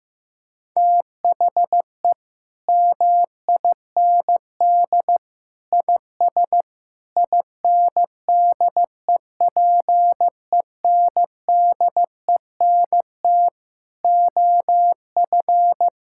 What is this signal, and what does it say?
Morse code, heard as a keyed tone, about 15 wpm, 700 Hz. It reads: THE MIND IS INDEPENDENT OF
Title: Sentences from Top 1000 Words - Review Part 6 - 15wpm